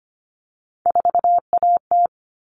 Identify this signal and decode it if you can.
Morse code, heard as a keyed tone, about 25 wpm, 700 Hz. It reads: 4AT
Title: All Cut Numbers - 3-Digit 25wpm